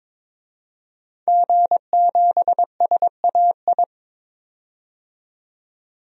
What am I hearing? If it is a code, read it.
Morse code, heard as a keyed tone, about 22 wpm, 700 Hz. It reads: G7SAI
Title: DXCC-100 Call Signs - 5 Characters - 22wpm